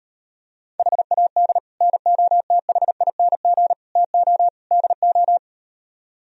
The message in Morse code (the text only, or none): HAD NOTHING TO DO